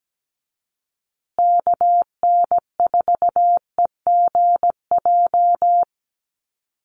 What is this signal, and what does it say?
Morse code, heard as a keyed tone, about 17 wpm, 700 Hz. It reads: KN4EGJ
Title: USA Call Signs Repeated 3x 17wpm